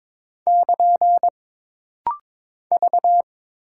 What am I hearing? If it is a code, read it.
Morse code, heard as a keyed tone, about 22 wpm, 700 Hz. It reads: <KN> V
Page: This practice set is composed of Letters, Numbers, common punctuations, and common prosigns.